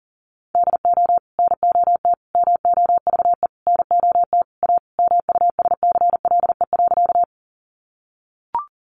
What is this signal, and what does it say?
Morse code, heard as a keyed tone, about 40 wpm, 700 Hz. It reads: DO NOT MOVE NOT A MUSCLE.
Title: Sentences from Top 1300 Words 40wpm